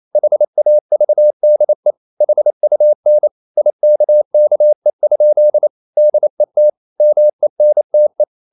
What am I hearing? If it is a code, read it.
Morse code, heard as a keyed tone, about 28 wpm, 600 Hz. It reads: HAVDE HUN IKKE? DET MENTE
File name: Ferie QRQ 28wpm